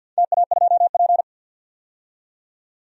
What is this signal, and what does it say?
Morse code, heard as a keyed tone, about 47 wpm, 700 Hz. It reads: TA2P
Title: DXCC-100 Call Signs - 4 Characters - 50wpm